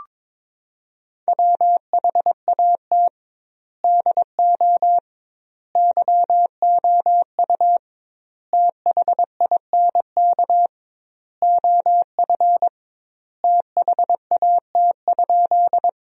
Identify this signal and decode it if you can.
Morse code, heard as a keyed tone, about 22 wpm, 700 Hz. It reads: WHAT DO YOU THINK OF THAT?
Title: 2x Word-Spacing - Sentences from Top 200 Words - 22wpm